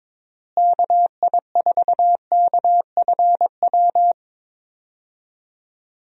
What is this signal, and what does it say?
Morse code, heard as a keyed tone, about 22 wpm, 700 Hz. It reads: KI4KFW
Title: USA Call Signs - 6 Characters - Repeated 3x - 22wpm